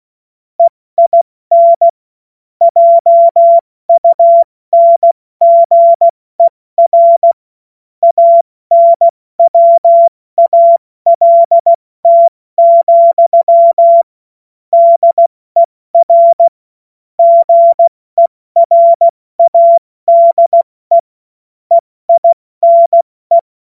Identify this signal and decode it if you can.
Morse code, heard as a keyed tone, about 16 wpm, 680 Hz. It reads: EIN JUNGER ANWALT, DER GERADE EINE
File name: Morsekurs / Witze / Witz 10. Mai 2022 70 F80